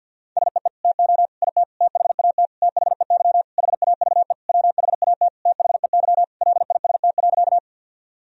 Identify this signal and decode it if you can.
Morse code, heard as a keyed tone, about 50 wpm, 700 Hz. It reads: SEE TO IT THAT THEY HAVE WHAT THEY LIST.
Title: Sentences from Top 1000 Words - Review Part 2 - 50wpm